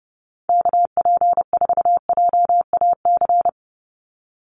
Morse code, heard as a keyed tone, about 28 wpm, 700 Hz. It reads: KP4JAC
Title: USA Call Signs 30wpm